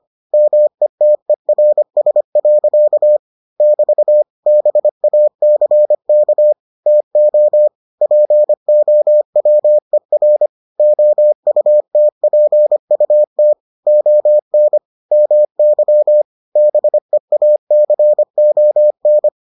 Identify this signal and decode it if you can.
Morse code, heard as a keyed tone, about 25 wpm, 600 Hz. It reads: METERS. <BT> BACK TO POWER OUTPUT ON MY BEACON